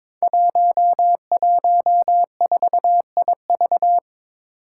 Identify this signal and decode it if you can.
Morse code, heard as a keyed tone, about 22 wpm, 700 Hz. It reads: 114IV
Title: Five Challenging Letter-Number Combinations 22wpm